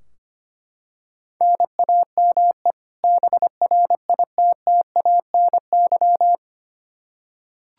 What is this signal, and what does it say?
Morse code, heard as a keyed tone, about 25 wpm, 700 Hz. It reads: NAME BRITTANY
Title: K1USN SST QSO Practice 25wpm